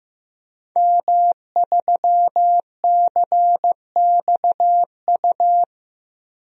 Morse code, heard as a keyed tone, about 15 wpm, 700 Hz. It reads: M3CXU
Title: DXCC-100 Call Signs - 5 Characters - Repeated 3x - 15wpm